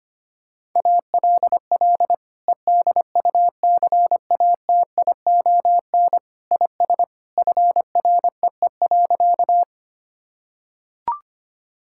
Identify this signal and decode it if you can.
Morse code, heard as a keyed tone, about 25 wpm, 700 Hz. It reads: ALL EDUCATION IS FREE.
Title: Sentences from Top 1000 Words - Review Part 6 - 25wpm